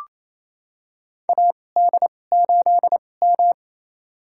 Morse code, heard as a keyed tone, about 28 wpm, spaced as for 20 wpm, 700 Hz. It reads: AD8M
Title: USA Call Signs - 4 Characters - 2x Character Spacing - 28wpm